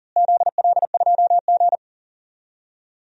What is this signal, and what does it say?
Morse code, heard as a keyed tone, about 40 wpm, 700 Hz. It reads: ZL2G
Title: DXCC-100 Call Signs - 4 Characters - Repeated 3x - 40wpm